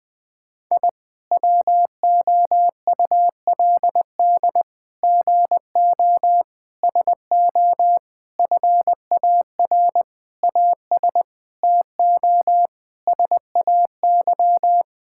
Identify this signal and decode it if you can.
Morse code, heard as a keyed tone, about 19 wpm, 700 Hz. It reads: I WOULD GO SO FAR AS TO SAY
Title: Sentences with 9-Word Phrases - 20wpm